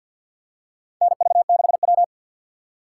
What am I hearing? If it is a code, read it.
Morse code, heard as a keyed tone, about 51 wpm, 700 Hz. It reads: NV6W
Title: USA Call Signs - 4 Characters - 2x Character Spacing - 50wpm